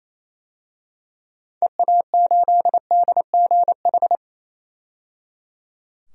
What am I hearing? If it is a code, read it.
Morse code, heard as a keyed tone, about 28 wpm, 700 Hz. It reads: EA8DGH
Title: DXCC-100 Call Signs - 6 Characters - 28wpm